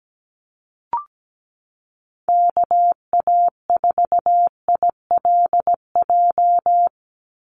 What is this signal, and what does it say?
Morse code, heard as a keyed tone, about 17 wpm, 700 Hz. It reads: KA4ILJ
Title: USA Call Signs 17wpm